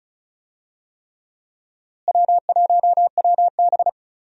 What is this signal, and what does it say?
Morse code, heard as a keyed tone, about 36 wpm, 700 Hz. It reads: W1WB
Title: USA Call Signs - 4 Characters - Repeated 3x - 35wpm